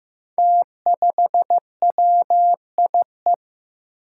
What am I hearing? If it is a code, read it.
Morse code, heard as a keyed tone, about 15 wpm, 700 Hz. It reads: T5WIE
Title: Five Challenging Letter-Number Combinations 15wpm